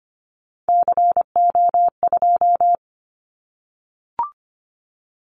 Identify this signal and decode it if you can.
Morse code, heard as a keyed tone, about 25 wpm, 700 Hz. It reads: CO2
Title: Callsign Prefixes for Top 100 DXCC - 25wpm